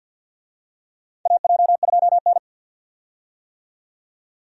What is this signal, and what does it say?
Morse code, heard as a keyed tone, about 51 wpm, 700 Hz. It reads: AJ2N
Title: USA Call Signs - 4 Characters - 50wpm